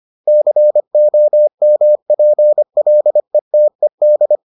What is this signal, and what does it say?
Morse code, heard as a keyed tone, about 25 wpm, 600 Hz. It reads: COMPLETED